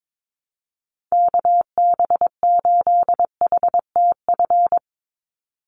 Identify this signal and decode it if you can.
Morse code, heard as a keyed tone, about 22 wpm, 700 Hz. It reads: KB8HTF
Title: USA Call Signs 22wpm